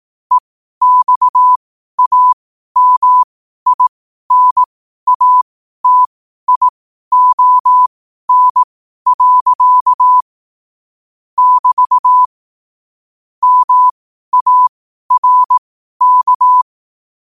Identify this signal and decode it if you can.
Morse code, heard as a keyed tone, about 18 wpm, spaced as for 12 wpm, 1000 Hz. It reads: EXAMINATION. <BT> MARK
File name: qotdfiles / 12wpm